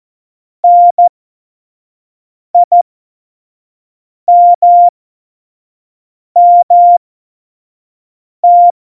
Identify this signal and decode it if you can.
Morse code, heard as a keyed tone, about 14 wpm, spaced as for 5 wpm, 700 Hz. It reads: NIMMT
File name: Morsekurs / Witze / Witz 22. Januar 2019, 25 F70